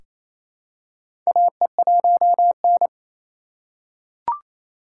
Morse code, heard as a keyed tone, about 28 wpm, 700 Hz. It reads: AE1N
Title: K1USN SST QSO Practice - 28wpm